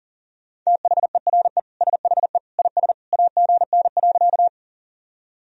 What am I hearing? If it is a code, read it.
Morse code, heard as a keyed tone, about 40 wpm, 700 Hz. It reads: THERE SHE IS AGN.
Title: Sentences from Top 200 Words and Abbreviations 40wpm